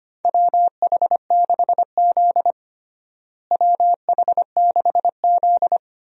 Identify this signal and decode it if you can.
Morse code, heard as a keyed tone, about 25 wpm, 700 Hz. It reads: WH6Z WH6Z
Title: USA Call Signs - 4 Characters - Repeated 2x - 25wpm